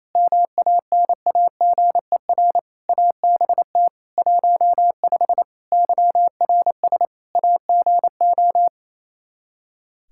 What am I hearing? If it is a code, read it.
Morse code, heard as a keyed tone, about 28 wpm, 700 Hz. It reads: MANAGER ABT 15 YRS AGO
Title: Practice QSO - Exchange 3 - More Info - 28wpm